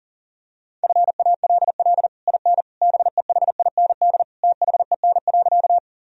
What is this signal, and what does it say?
Morse code, heard as a keyed tone, about 40 wpm, 700 Hz. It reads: FALL IN BEHIND THEN.
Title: Sentences from Top 1000 Words - Review Part 2 - 40wpm